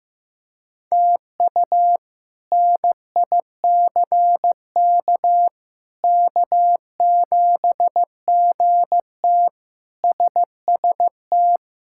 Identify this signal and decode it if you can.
Morse code, heard as a keyed tone, about 15 wpm, 700 Hz. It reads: TU NICK K7GT SST
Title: K1USN SST QSO Practice - 15wpm